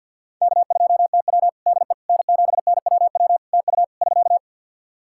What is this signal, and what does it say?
Morse code, heard as a keyed tone, about 50 wpm, 700 Hz. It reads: K2TW DE N7NWW TU <SK>
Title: K1USN SST QSO Practice 50wpm